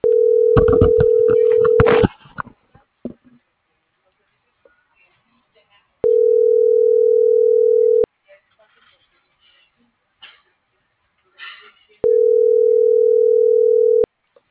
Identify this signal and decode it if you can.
Ringback tone (North America), 440 + 480 Hz.